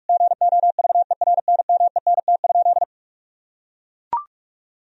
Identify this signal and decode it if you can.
Morse code, heard as a keyed tone, about 41 wpm, 700 Hz. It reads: GOVERNMENT?
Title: Sentences from Top 1000 Words - Review Part 4 - 45wpm